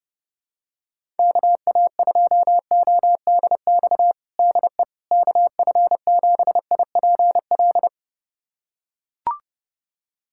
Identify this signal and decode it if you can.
Morse code, heard as a keyed tone, about 30 wpm, 700 Hz. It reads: KA2ODX DE KF7IPL
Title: USA Callsign DE Callsign - 6 Characters - 30wpm